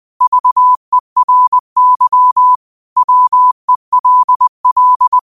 Morse code, heard as a keyed tone, about 20 wpm, 1000 Hz. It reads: VERY WELL